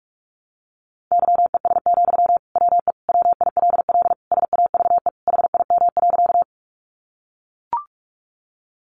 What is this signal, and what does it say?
Morse code, heard as a keyed tone, about 44 wpm, 700 Hz. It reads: YES, WE WILL SAVE HIM.
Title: Sentences from Top 1100 Words 45wpm